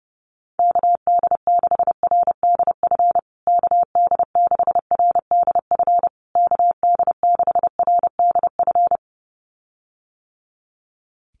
Morse code, heard as a keyed tone, about 30 wpm, 700 Hz. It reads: KD6RDF KD6RDF KD6RDF
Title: USA Call Signs Repeated 3x 30wpm